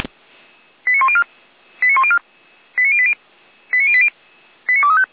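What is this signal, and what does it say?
Five-tone selective call (ZVEI), heard as quick runs of five tones, tones of about 70 ms. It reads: 89104 89104 89080 89080 89235
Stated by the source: ZVEI five-tone selective call received by SDR.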